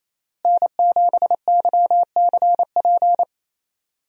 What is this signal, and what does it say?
Morse code, heard as a keyed tone, about 28 wpm, 700 Hz. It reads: N7YCP
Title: USA Call Signs - 5 Characters - Repeated 2x - 28wpm